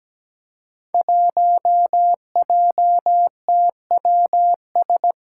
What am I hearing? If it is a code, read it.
Morse code, heard as a keyed tone, about 17 wpm, 700 Hz. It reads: 1JTWS
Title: Five Challenging Letter-Number Combinations 17wpm